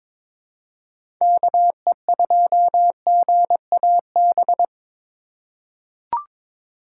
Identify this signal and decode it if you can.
Morse code, heard as a keyed tone, about 22 wpm, 700 Hz. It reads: KE2GAB
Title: USA Call Signs - 6 Characters - 2x Character Spacing - 22wpm